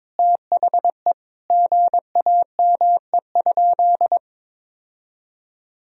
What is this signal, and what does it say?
Morse code, heard as a keyed tone, about 21 wpm, 700 Hz. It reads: THE GAME?
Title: Sentences from Top 1000 Words - Review Part 4 - 22wpm